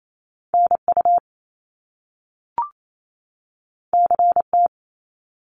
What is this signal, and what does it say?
Morse code, heard as a keyed tone, about 28 wpm, 700 Hz. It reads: NU CT
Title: Top Two Letter Combinations in Words 28wpm